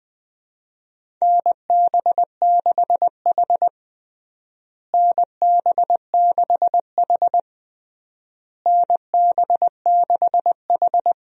Morse code, heard as a keyed tone, about 21 wpm, 700 Hz. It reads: NB6H NB6H NB6H
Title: USA Call Signs - 4 Characters - Repeated 3x - 20wpm